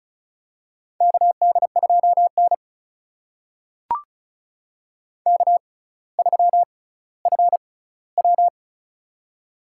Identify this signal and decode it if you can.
Morse code, heard as a keyed tone, about 34 wpm, spaced as for 18 wpm, 700 Hz. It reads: KD2N K3FW
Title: USA Call Signs - 4 Characters - 6x Character Spacing - 35wpm